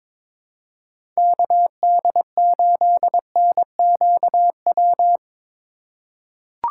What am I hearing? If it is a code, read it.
Morse code, heard as a keyed tone, about 22 wpm, 700 Hz. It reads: KD8NQW
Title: USA Call Signs - 6 Characters - 3x Character Spacing - 22wpm